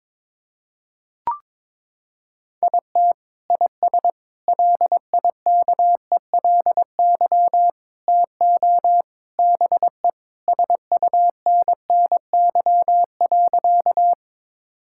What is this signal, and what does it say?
Morse code, heard as a keyed tone, about 22 wpm, 700 Hz. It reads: IT IS LIKELY TO BE SUNNY.
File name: practice / 22wpm / Sentences with 3-Word Phrases - 22wpm